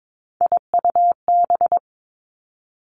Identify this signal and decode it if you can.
Morse code, heard as a keyed tone, about 22 wpm, 700 Hz. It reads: IUB
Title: Three Challenging Letter-Number Combinations 22wpm